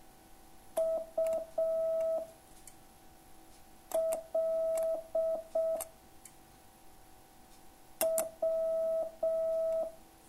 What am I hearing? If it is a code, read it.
Morse code, heard as a keyed tone, about 6 wpm, 660 Hz. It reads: U L W